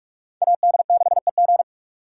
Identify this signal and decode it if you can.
Morse code, heard as a keyed tone, about 68 wpm, spaced as for 56 wpm, 700 Hz. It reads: AD6EG
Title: Speed Racing - USA 2x2 Call Signs 45wpm